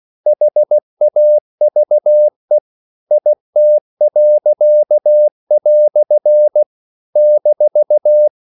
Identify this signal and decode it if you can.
Morse code, heard as a keyed tone, about 16 wpm, 600 Hz. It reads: HAVE IT." -
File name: CW / Practice / EnglishQuotations0116